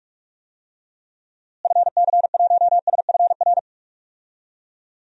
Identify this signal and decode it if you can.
Morse code, heard as a keyed tone, about 46 wpm, 700 Hz. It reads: FC1SFR
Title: DXCC-100 Call Signs - 6 Characters - Repeated 3x - 45wpm